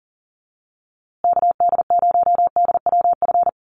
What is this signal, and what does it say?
Morse code, heard as a keyed tone, about 40 wpm, 700 Hz. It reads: KD0DWF
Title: USA Call Signs Repeated 2x 40wpm